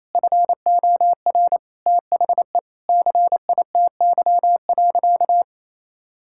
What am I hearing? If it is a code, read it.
Morse code, heard as a keyed tone, about 28 wpm, 700 Hz. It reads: FOR THE CITY.
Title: Sentences from Top 1600 Words 28wpm